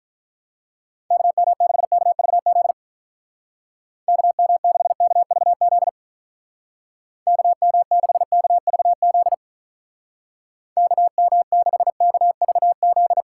Morse code, heard as a keyed tone, about 44 wpm, 700 Hz. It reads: KM6KVZ KM6KVZ KM6KVZ KM6KVZ
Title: Speed Racing - USA 2x3 Call Signs 35wpm